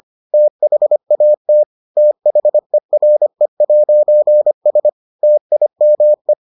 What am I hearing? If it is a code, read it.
Morse code, heard as a keyed tone, about 25 wpm, 600 Hz. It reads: THAT THERE'S TIME